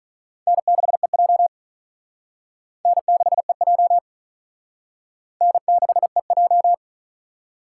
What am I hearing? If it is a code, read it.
Morse code, heard as a keyed tone, about 43 wpm, 700 Hz. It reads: N6EJ N6EJ N6EJ
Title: Speed Racing - USA Call Signs 35wpm